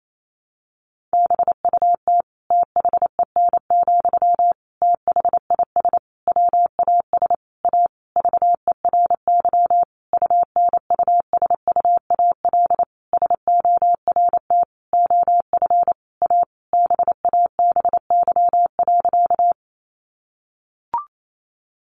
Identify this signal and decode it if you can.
Morse code, heard as a keyed tone, about 28 wpm, 700 Hz. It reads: BUT THEN, THIS WAS A VERY UNUSUAL SORT OF A BABY.
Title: Sentences from Top 1400 Words 28wpm